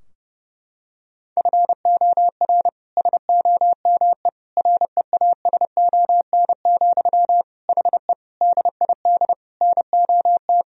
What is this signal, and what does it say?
Morse code, heard as a keyed tone, about 30 wpm, 700 Hz. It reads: FOR SOME REASON, HE DID NOT
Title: Sentences from Top 1000 Words - Review Part 3 - 30wpm